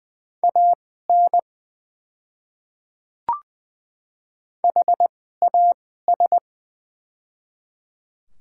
Morse code, heard as a keyed tone, about 20 wpm, spaced as for 15 wpm, 700 Hz. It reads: AN HAS
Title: 2x Character Spacing - Top 100 Words - 20wpm